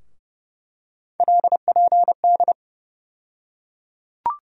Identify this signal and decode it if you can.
Morse code, heard as a keyed tone, about 30 wpm, 700 Hz. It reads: LPD
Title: USA Call Sign Postfixes - 3 Letters 30wpm